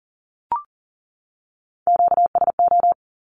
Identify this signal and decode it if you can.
Morse code, heard as a keyed tone, about 40 wpm, 700 Hz. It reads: QSO
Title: Q-Codes 40wpm